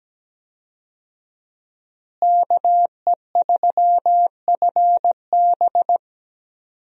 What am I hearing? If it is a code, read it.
Morse code, heard as a keyed tone, about 17 wpm, 700 Hz. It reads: KE3FB
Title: USA Call Signs - 5 Characters - Repeated 3x - 17wpm